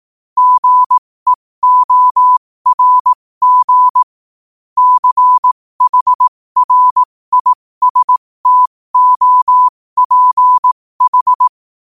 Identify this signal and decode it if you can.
Morse code, heard as a keyed tone, about 18 wpm, spaced as for 15 wpm, 1000 Hz. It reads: GEORG CHRISTOPH
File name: qotdfiles / 15wpm